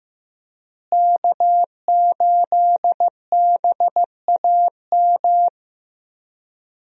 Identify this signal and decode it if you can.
Morse code, heard as a keyed tone, about 15 wpm, 700 Hz. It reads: K8BAM
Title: USA Call Signs - 5 Characters - Repeated 2x - 15wpm